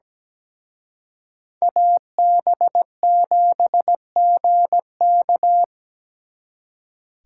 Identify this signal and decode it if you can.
Morse code, heard as a keyed tone, about 20 wpm, 700 Hz. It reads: AB7GK
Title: Speed Racing - USA 2x2 Call Signs 17wpm